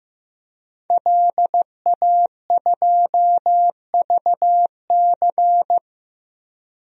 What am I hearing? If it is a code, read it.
Morse code, heard as a keyed tone, about 15 wpm, 700 Hz. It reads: LA2VC
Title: DXCC-100 Call Signs - 5 Characters - 15wpm